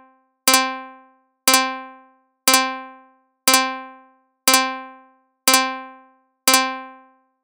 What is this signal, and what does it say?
There is no code.